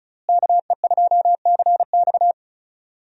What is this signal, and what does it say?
Morse code, heard as a keyed tone, about 36 wpm, 700 Hz. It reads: KE2CX
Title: USA Call Signs - 5 Characters - Repeated 2x - 35wpm